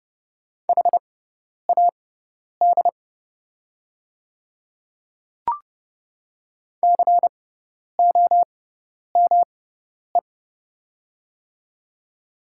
Morse code, heard as a keyed tone, about 30 wpm, spaced as for 10 wpm, 690 Hz. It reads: HAD COME
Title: 6x Character Spacing - Top 100 Words - 30wpm